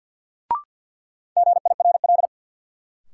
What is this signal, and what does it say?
Morse code, heard as a keyed tone, about 46 wpm, 700 Hz. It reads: GIRL
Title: Sentences from Top 400 Words - New Word Drill - 50wpm